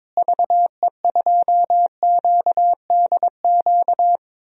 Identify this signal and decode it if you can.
Morse code, heard as a keyed tone, about 22 wpm, 700 Hz. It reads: VE2QDQ
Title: DXCC-100 Call Signs - 6 Characters - Repeated 2x - 22wpm